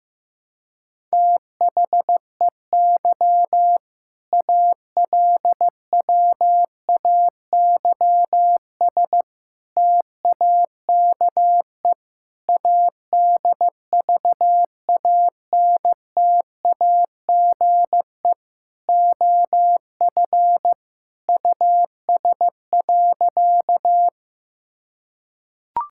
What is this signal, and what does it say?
Morse code, heard as a keyed tone, about 15 wpm, 700 Hz. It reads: THEY ALWAYS TAKE ADVANTAGE OF US.
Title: Sentences from Top 1000 Words - Review Part 6 - 15wpm